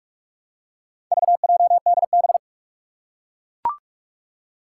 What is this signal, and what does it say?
Morse code, heard as a keyed tone, about 44 wpm, 700 Hz. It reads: VJDB
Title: Four Challenging Letter-Number Combinations 45wpm